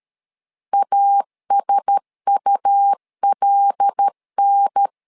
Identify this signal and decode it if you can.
Morse code, heard as a keyed tone, about 13 wpm, 800 Hz. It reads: ASULN
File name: LEZIONI / 5 LBVC / 5 LB 12wpm 1906